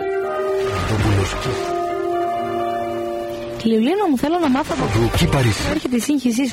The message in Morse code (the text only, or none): none